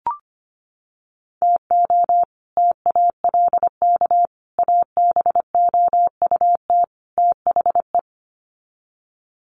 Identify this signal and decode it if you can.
Morse code, heard as a keyed tone, about 25 wpm, 700 Hz. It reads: TO TALK ABOUT THE
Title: Sets of 4 Words 25wpm